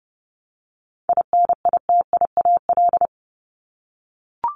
Morse code, heard as a keyed tone, about 30 wpm, 700 Hz. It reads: INITIAL
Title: Top 1501-1750 Words 30wpm